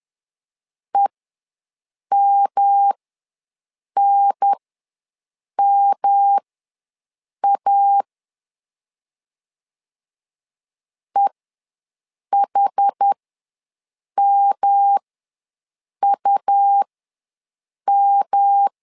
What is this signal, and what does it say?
Morse code, heard as a keyed tone, about 10 wpm, spaced as for 6 wpm, 800 Hz. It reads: EMNMA EHMUM
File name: LEZIONI / 3 MWRO / 3 M ws 10wpm 2121